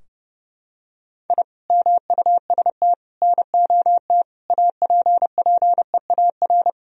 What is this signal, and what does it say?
Morse code, heard as a keyed tone, about 30 wpm, 700 Hz. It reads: I MUST NOT APPEAR
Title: Sentences from Top 1000 Words - Review Part 2 - 30wpm